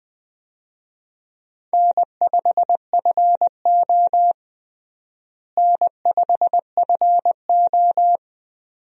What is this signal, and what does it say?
Morse code, heard as a keyed tone, about 20 wpm, 700 Hz. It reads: N5FO N5FO
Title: USA Call Signs - 4 Characters - Repeated 2x - 20wpm